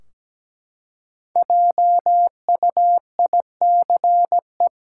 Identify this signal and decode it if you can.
Morse code, heard as a keyed tone, about 17 wpm, 700 Hz. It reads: JUICE
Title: Five Letter Words - 17wpm